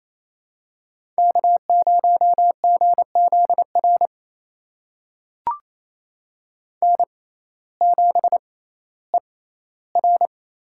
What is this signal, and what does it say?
Morse code, heard as a keyed tone, about 28 wpm, spaced as for 15 wpm, 700 Hz. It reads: K0GZR N7ER
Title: USA Call Signs - 6x Character Spacing - 28wpm